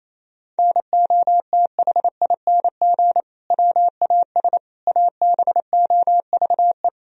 Morse code, heard as a keyed tone, about 28 wpm, 700 Hz. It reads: NOTHING WAS ABOVE
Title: Sentences from Top 1000 Words - Review Part 2 - 28wpm